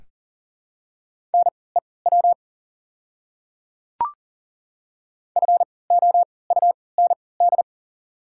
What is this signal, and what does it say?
Morse code, heard as a keyed tone, about 39 wpm, spaced as for 23 wpm, 690 Hz. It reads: NEW FOUND
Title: 3x Character Spacing - Top 500 Words - 40wpm